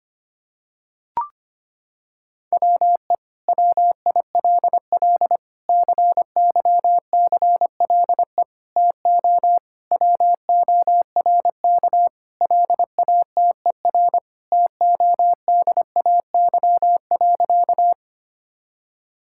Morse code, heard as a keyed tone, about 25 wpm, 700 Hz. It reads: WE WILL CYCLE TO WORK LATER TODAY.
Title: Sentences from Top 1000 Words - Review Part 6 - 25wpm